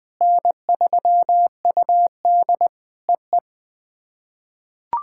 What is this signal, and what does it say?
Morse code, heard as a keyed tone, about 20 wpm, 700 Hz. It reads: N3UD EE
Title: POTA QSO Practice - 20wpm